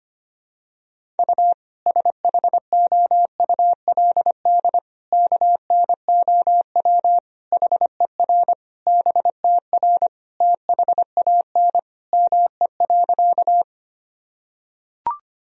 Morse code, heard as a keyed tone, about 25 wpm, 700 Hz. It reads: U SHOULD KNOW HER BTR THAN ME.
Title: Sentences from Top 400 Words and Abbreviations 25wpm